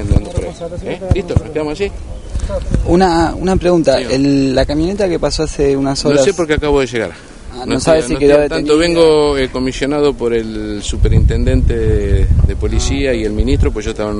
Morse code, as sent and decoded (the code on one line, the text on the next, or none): none